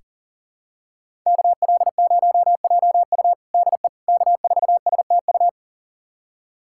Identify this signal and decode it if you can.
Morse code, heard as a keyed tone, about 40 wpm, 700 Hz. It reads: KL0JU DE K4STU
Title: USA Callsign DE Callsign - 5 Characters - 40wpm